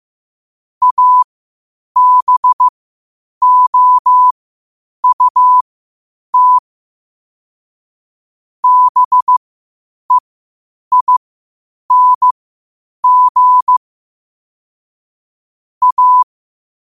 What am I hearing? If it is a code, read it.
Morse code, heard as a keyed tone, about 15 wpm, spaced as for 8 wpm, 1000 Hz. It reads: ABOUT BEING A
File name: qotdfiles / 8wpm